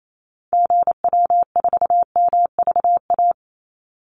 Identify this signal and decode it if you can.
Morse code, heard as a keyed tone, about 28 wpm, 700 Hz. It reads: GW4MVA
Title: Call Signs - 28wpm